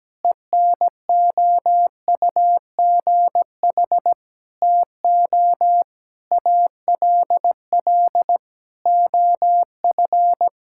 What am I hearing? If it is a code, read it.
Morse code, heard as a keyed tone, about 17 wpm, 700 Hz. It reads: ENOUGH TO ALL OF
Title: Sentences from Top 1000 Words - Review Part 5 - 17wpm